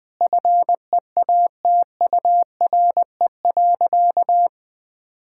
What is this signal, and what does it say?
Morse code, heard as a keyed tone, about 20 wpm, 700 Hz. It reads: FEATURE.
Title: Sentences from Top 1000 Words - Review Part 5 - 20wpm